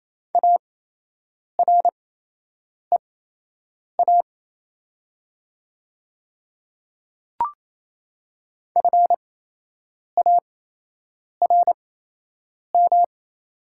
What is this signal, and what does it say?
Morse code, heard as a keyed tone, about 28 wpm, spaced as for 8 wpm, 700 Hz. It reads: AREA FARM
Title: 8x Character Spacing - Top 500 Words - 28wpm